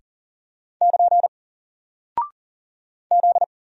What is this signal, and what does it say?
Morse code, heard as a keyed tone, about 40 wpm, 700 Hz. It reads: <KN> Z
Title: Single Character (Letter:Number:Punctuation:Prosign) Alt 40wpm